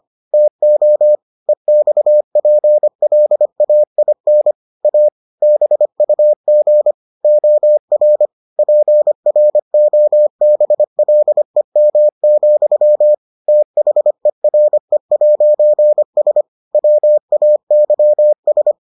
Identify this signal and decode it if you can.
Morse code, heard as a keyed tone, about 25 wpm, 600 Hz. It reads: TO EXPLAIN A BUG OR PROBLEM, THERE'S WAYS